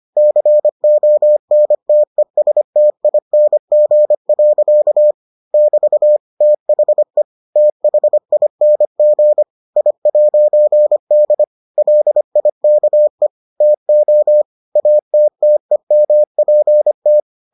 Morse code, heard as a keyed tone, about 25 wpm, 600 Hz. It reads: CONTESTING. <BT> THE THING I'D LIKE TO ATTEMPT